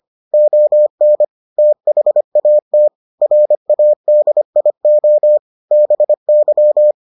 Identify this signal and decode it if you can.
Morse code, heard as a keyed tone, about 25 wpm, 600 Hz. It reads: ON THAT RADIO BY